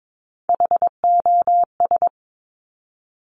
Morse code, heard as a keyed tone, about 22 wpm, 700 Hz. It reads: HOS
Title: Top Three Letter Combinations in Words 22wpm